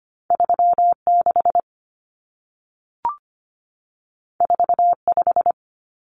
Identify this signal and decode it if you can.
Morse code, heard as a keyed tone, about 25 wpm, 700 Hz. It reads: 36 45
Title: Numbers - 2-Digit 25wpm